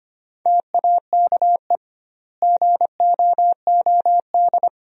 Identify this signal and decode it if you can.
Morse code, heard as a keyed tone, about 25 wpm, 700 Hz. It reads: TAKE GOOD
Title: 2x Word-Spacing - Sentences from Top 500 Words - 25wpm